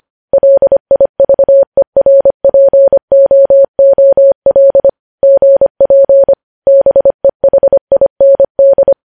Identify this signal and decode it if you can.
Morse code, heard as a keyed tone, about 25 wpm, 550 Hz. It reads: LIVERPOOL GP BEHIND
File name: podcast / News headlines-1x-25x25